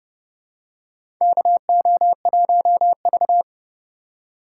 Morse code, heard as a keyed tone, about 30 wpm, 700 Hz. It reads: KO1V
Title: USA Call Signs - 4x Character Spacing - 30wpm